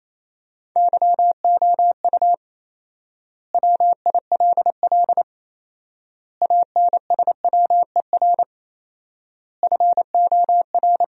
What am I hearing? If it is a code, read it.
Morse code, heard as a keyed tone, about 28 wpm, 700 Hz. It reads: YOU WILL ANSWER FOR
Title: 4x Word-Spacing - Sentences from Top 300 Words - 28wpm